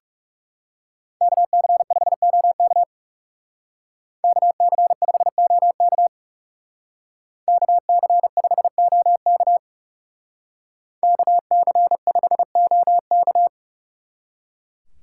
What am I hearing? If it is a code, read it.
Morse code, heard as a keyed tone, about 35 wpm, 700 Hz. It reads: KC5OK KC5OK KC5OK KC5OK
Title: Speed Racing - USA 2x2 Call Signs 30wpm